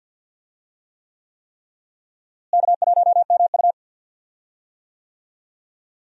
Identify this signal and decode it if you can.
Morse code, heard as a keyed tone, about 53 wpm, 700 Hz. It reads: K1MU
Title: USA Call Signs - 4 Characters - Repeated 3x - 50wpm